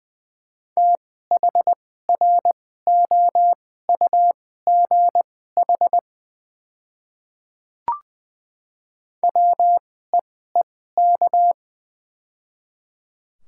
Morse code, heard as a keyed tone, about 20 wpm, spaced as for 14 wpm, 700 Hz. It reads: THROUGH WEEK
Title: 2x Character Spacing - Top 500 Words - 20wpm